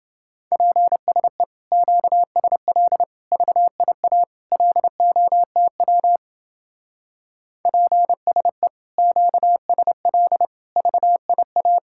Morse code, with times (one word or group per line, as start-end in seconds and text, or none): 0.5-1.4 PSE
1.7-3.0 QSL
3.3-4.2 VIA
4.5-6.2 LOTW
7.7-8.7 PSE
9.0-10.5 QSL
10.8-11.8 VIA